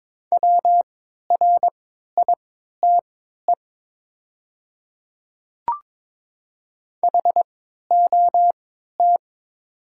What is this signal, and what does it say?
Morse code, heard as a keyed tone, about 22 wpm, spaced as for 13 wpm, 700 Hz. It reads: WRITE HOT
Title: 3x Character Spacing - Top 100 Words - 22wpm